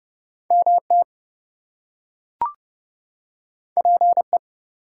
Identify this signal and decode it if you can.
Morse code, heard as a keyed tone, about 30 wpm, 700 Hz. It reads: MT PE
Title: US States and Canadian Provinces 30wpm